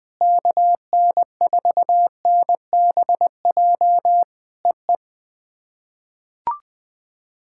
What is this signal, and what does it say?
Morse code, heard as a keyed tone, about 20 wpm, 700 Hz. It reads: KN4NBJ EE
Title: POTA QSO Practice - 20wpm